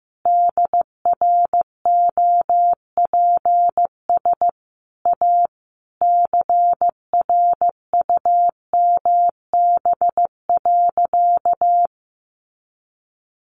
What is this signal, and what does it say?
Morse code, heard as a keyed tone, about 15 wpm, 700 Hz. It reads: DROPS A CRUMB.
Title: Additional Easy Phrases 15wpm